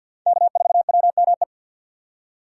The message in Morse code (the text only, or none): K4WME